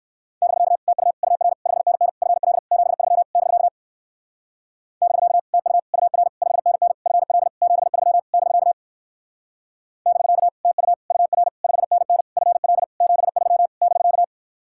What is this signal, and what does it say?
Morse code, heard as a keyed tone, about 59 wpm, 700 Hz. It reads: <BK> TU UR 5NN FL 73 <BK> <BK> TU UR 5NN FL 73 <BK> <BK> TU UR 5NN FL 73 <BK>